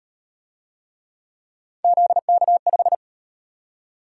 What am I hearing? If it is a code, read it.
Morse code, heard as a keyed tone, about 33 wpm, 700 Hz. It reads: ZK5